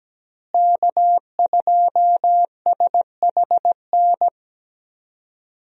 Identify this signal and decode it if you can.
Morse code, heard as a keyed tone, about 17 wpm, 700 Hz. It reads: K2SHN
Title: USA Call Signs - 5 Characters - Repeated 2x - 17wpm